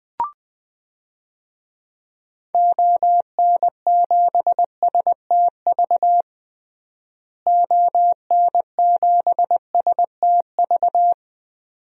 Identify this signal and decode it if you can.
Morse code, heard as a keyed tone, about 21 wpm, 700 Hz. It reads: ON7STV ON7STV
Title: DXCC-100 Call Signs - 6 Characters - Repeated 2x - 20wpm